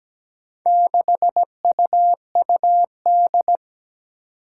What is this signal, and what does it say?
Morse code, heard as a keyed tone, about 17 wpm, 700 Hz. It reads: 6UUD
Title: Four Challenging Letter-Number Combinations 17wpm